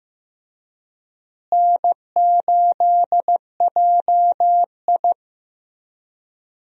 Morse code, heard as a keyed tone, about 15 wpm, 700 Hz. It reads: N8JI
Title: USA Call Signs - 4 Characters - Repeated 3x - 15wpm